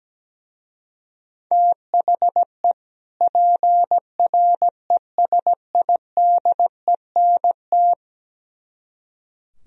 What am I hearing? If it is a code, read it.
Morse code, heard as a keyed tone, about 17 wpm, 700 Hz. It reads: THE PRESIDENT
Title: Speed-Racing - Sets of 2 Words 17wpm